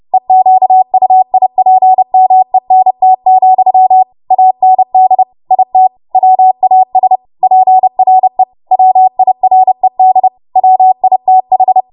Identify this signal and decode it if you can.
Morse code, heard as a keyed tone, about 30 wpm, 750 Hz. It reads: EQUIPMENT, AND IT WAS PRE WIRED WITH
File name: Morse / 30WPM